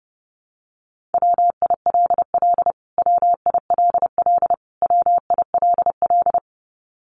Morse code, heard as a keyed tone, about 28 wpm, 700 Hz. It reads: WILL WILL WILL
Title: Top 500 Words Repeated 30wpm